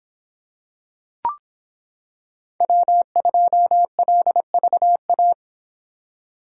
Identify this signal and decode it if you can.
Morse code, heard as a keyed tone, about 26 wpm, 700 Hz. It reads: W2LVA